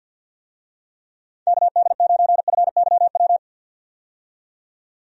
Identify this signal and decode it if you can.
Morse code, heard as a keyed tone, about 48 wpm, 700 Hz. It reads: KD9FYW
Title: USA Call Signs - 6 Characters - 50wpm